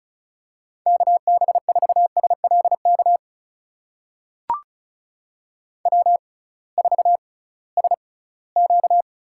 Morse code, heard as a keyed tone, about 34 wpm, spaced as for 20 wpm, 700 Hz. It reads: KB4SLK W4SQ
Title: USA Call Signs - 6x Character Spacing - 35wpm